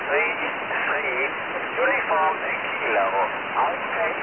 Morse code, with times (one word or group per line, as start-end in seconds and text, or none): none